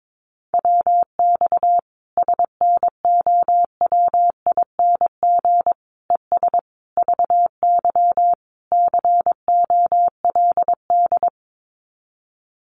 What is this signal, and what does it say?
Morse code, heard as a keyed tone, about 22 wpm, 700 Hz. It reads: WX SNOWING ES VY COLD
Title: Common QSO Phrases 22wpm